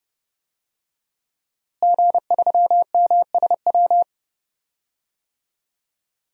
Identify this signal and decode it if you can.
Morse code, heard as a keyed tone, about 32 wpm, 700 Hz. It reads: G3MSW
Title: DXCC-100 Call Signs - 5 Characters - Repeated 3x - 30wpm